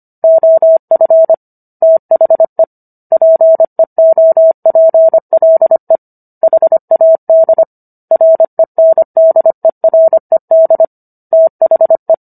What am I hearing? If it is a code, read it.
Morse code, heard as a keyed tone, about 25 wpm, 650 Hz. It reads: OF THE PEOPLE HAD RENDERED THE
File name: Book One / 16-Chapter 16 - 25 WPM